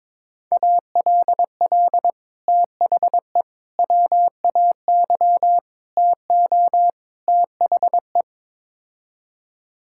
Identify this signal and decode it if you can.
Morse code, heard as a keyed tone, about 22 wpm, 700 Hz. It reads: ALL THE WAY TO THE
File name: practice / 22wpm / Build A Phrase - 5-Word Phrases - 22wpm